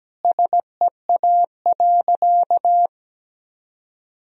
Morse code, heard as a keyed tone, about 17 wpm, 700 Hz. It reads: SEA.
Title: Sentences from Top 1000 Words - Review Part 2 - 17wpm